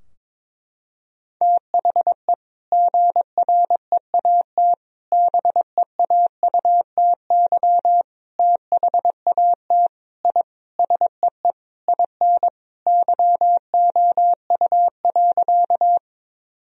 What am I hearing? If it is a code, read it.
Morse code, heard as a keyed tone, about 22 wpm, 700 Hz. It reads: THE GREAT BEAUTY THAT I SEE IN YOU.
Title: Sentences from Top 1000 Words - Review Part 2 - 22wpm